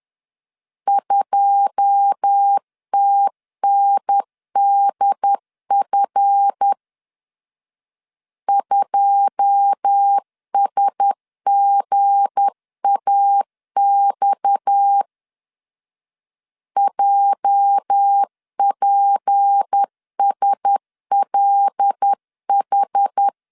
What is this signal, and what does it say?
Morse code, heard as a keyed tone, about 10 wpm, 800 Hz. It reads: 2TNDF 2SGAX JPSLH
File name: LEZIONI / 9 2 10wpm 2736